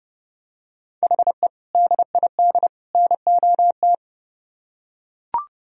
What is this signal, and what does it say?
Morse code, heard as a keyed tone, about 30 wpm, 700 Hz. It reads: HE DID NOT